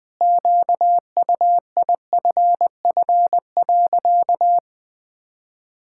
Morse code, heard as a keyed tone, about 20 wpm, 700 Hz. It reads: QUIFF.
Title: Pangram Challenge v2 - 20wpm